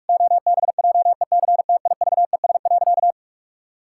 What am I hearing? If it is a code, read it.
Morse code, heard as a keyed tone, about 46 wpm, 700 Hz. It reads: OBJECTIVES.
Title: Pangram Challenge v2 - 45wpm